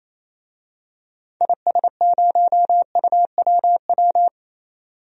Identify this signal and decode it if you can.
Morse code, heard as a keyed tone, about 27 wpm, 700 Hz. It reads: IS0UWW
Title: DXCC-100 Call Signs - 6 Characters - 28wpm